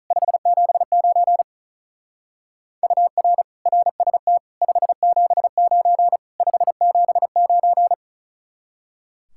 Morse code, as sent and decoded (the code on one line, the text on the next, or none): ..... --... ----. / ..- .-. / .-. ... - / ..... --... ----. / ..... --... ----.
579 UR RST 579 579